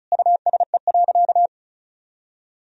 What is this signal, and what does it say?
Morse code, heard as a keyed tone, about 36 wpm, 700 Hz. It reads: USE.
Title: Sentences from Top 1000 Words - Review Part 4 - 35wpm